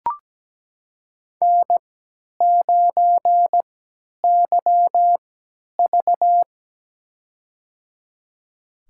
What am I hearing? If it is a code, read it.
Morse code, heard as a keyed tone, about 17 wpm, spaced as for 10 wpm, 700 Hz. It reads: N9YV
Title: USA Call Signs - 4 Characters - 3x Character Spacing - 17wpm